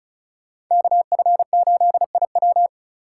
Morse code, heard as a keyed tone, about 44 wpm, 700 Hz. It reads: KF8IW